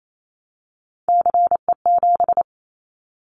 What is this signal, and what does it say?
Morse code, heard as a keyed tone, about 28 wpm, 700 Hz. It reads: CE7
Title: Callsign Prefixes for Top 100 DXCC - 28wpm